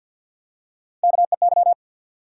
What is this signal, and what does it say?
Morse code, heard as a keyed tone, about 53 wpm, 700 Hz. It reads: KEY
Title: Three Letter Words - 50wpm